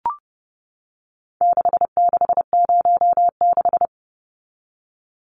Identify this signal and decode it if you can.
Morse code, heard as a keyed tone, about 30 wpm, 700 Hz. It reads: 6606
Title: Numbers - 4-Digit 30wpm